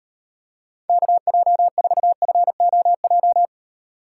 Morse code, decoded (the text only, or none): KJ4FOJ